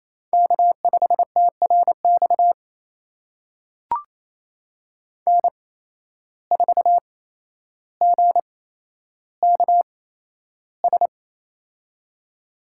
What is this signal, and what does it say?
Morse code, heard as a keyed tone, about 27 wpm, spaced as for 12 wpm, 700 Hz. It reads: K5TRX N4GKS
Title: USA Call Signs - 5 Characters - 8x Character Spacing - 28wpm